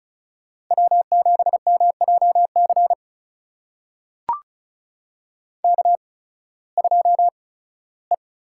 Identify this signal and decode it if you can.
Morse code, heard as a keyed tone, about 36 wpm, spaced as for 19 wpm, 700 Hz. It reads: W7MJC K2E
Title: USA Call Signs - 5 Characters - 8x Character Spacing - 35wpm